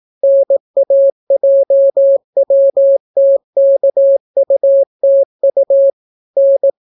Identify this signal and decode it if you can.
Morse code, heard as a keyed tone, about 18 wpm, 550 Hz. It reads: NAJWTKUTU N